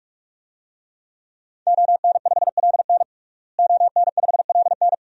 Morse code, heard as a keyed tone, about 45 wpm, 700 Hz. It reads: ON5LN ON5LN
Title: DXCC-100 Call Signs - 5 Characters - Repeated 2x - 45wpm